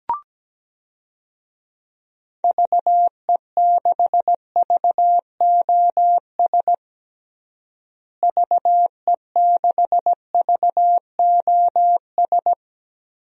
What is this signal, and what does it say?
Morse code, heard as a keyed tone, about 17 wpm, 700 Hz. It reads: VE6VOS VE6VOS
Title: DXCC-100 Call Signs - 6 Characters - Repeated 2x - 17wpm